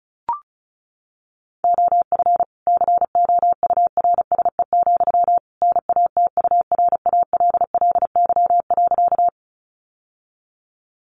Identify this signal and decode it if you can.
Morse code, heard as a keyed tone, about 34 wpm, 700 Hz. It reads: OF COURSE, NATURALLY.
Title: Sentences from Top 1300 Words 35wpm